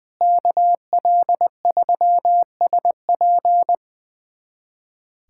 Morse code, heard as a keyed tone, about 20 wpm, 700 Hz. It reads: KL3SP
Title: USA Callsign DE Callsign - 5 Characters - 20wpm